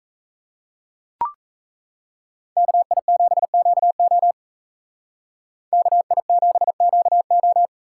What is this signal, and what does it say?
Morse code, heard as a keyed tone, about 36 wpm, 700 Hz. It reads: KI7QO KI7QO